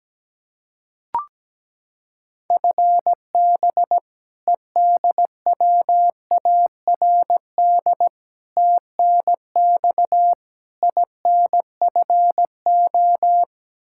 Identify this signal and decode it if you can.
Morse code, heard as a keyed tone, about 17 wpm, 700 Hz. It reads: FB EDWARD TNX INFO
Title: Practice QSO - Exchange 3 - More Info - 17wpm